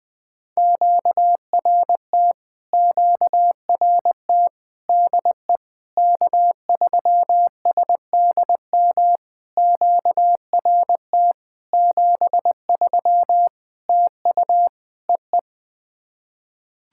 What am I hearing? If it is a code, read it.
Morse code, heard as a keyed tone, about 21 wpm, 700 Hz. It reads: QRT QRT DE K3SDM QRT 73 TU EE
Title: POTA QSO Practice - 20wpm